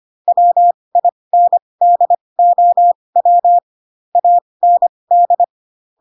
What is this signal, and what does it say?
Morse code, heard as a keyed tone, about 25 wpm, spaced as for 20 wpm, 700 Hz. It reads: WINDOW AND
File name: Pinocchio / Pinocchio ch6 15 25 ik1hge